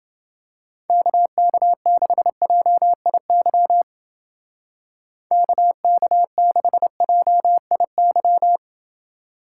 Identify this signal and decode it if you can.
Morse code, heard as a keyed tone, about 28 wpm, 700 Hz. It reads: KK6JIY KK6JIY